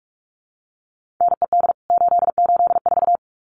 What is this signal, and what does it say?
Morse code, heard as a keyed tone, about 44 wpm, 700 Hz. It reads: NED 884
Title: CWT Exchange 45wpm